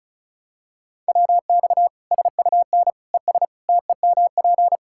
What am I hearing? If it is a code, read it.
Morse code, heard as a keyed tone, about 36 wpm, 700 Hz. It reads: WX SUN ES TEMP
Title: Practice QSO - Exchange 2 - Info - 35wpm